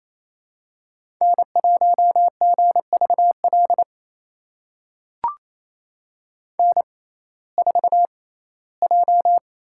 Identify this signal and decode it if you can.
Morse code, heard as a keyed tone, about 28 wpm, spaced as for 17 wpm, 700 Hz. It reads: N1GVL N4J
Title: USA Call Signs - 5 Characters - 6x Character Spacing - 28wpm